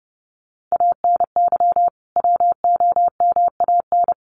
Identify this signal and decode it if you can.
Morse code, heard as a keyed tone, about 30 wpm, 700 Hz. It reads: ANY WOMAN
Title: Sentences from Top 1100 Words 30wpm